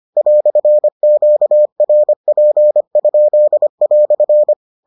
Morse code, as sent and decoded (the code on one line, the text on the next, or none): .-..-. --.- .-. .--. ..--.. .-..-.
"QRP?"